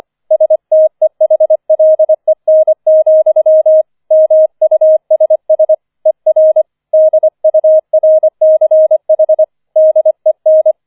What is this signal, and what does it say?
Morse code, heard as a keyed tone, about 25 wpm, 630 Hz. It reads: STEHLEN, MUSS ER DURCH DEN